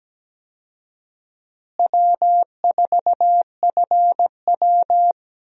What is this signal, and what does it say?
Morse code, heard as a keyed tone, about 17 wpm, 700 Hz. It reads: W4FW
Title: USA Call Signs - 4 Characters - Repeated 2x - 17wpm